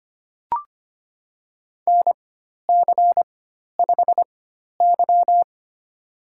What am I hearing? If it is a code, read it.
Morse code, heard as a keyed tone, about 25 wpm, spaced as for 12 wpm, 700 Hz. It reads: NC5Y
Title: USA Call Signs - 4 Characters - 4x Character Spacing - 25wpm